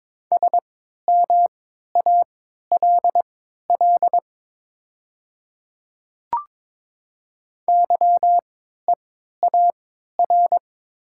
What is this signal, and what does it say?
Morse code, heard as a keyed tone, about 22 wpm, spaced as for 12 wpm, 700 Hz. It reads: SMALL YEAR
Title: 3x Character Spacing - Top 300 Words - 22wpm